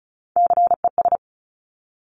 Morse code, heard as a keyed tone, about 36 wpm, 700 Hz. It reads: CES
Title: Top Three Letter Combinations in Words 35wpm